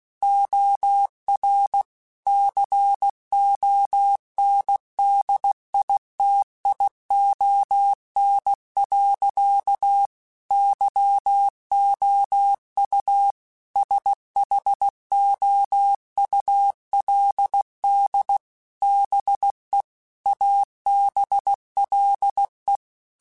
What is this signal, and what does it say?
Morse code, heard as a keyed tone, about 16 wpm, 800 Hz. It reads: OR CONDITION. YOU SHOULD BE ABLE
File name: Technical Subjects / Technical Subjects-mp3s / OR Gate @15